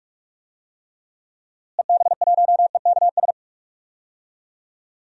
Morse code, heard as a keyed tone, about 42 wpm, 700 Hz. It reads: EB1EKS
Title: DXCC-100 Call Signs - 6 Characters - Repeated 2x - 45wpm